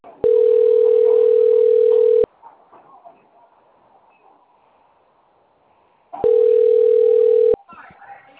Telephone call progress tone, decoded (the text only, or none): ringback tone